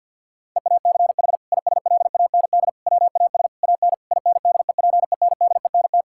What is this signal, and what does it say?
Morse code, heard as a keyed tone, about 51 wpm, 700 Hz. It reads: EACH ISLAND WAS AN INDEPENDENT
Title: Sentences from Top 1000 Words - Review Part 6 - 50wpm